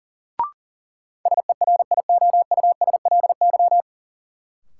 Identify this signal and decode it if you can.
Morse code, heard as a keyed tone, about 40 wpm, 700 Hz. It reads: SERIOUSLY
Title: Sentences from Top 1200 Words - New Word Drill Alt - 40wpm